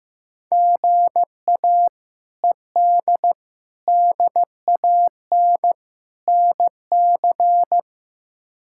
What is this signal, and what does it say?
Morse code, heard as a keyed tone, about 15 wpm, 700 Hz. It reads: GA ED DAN NC
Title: K1USN SST QSO Practice - 15wpm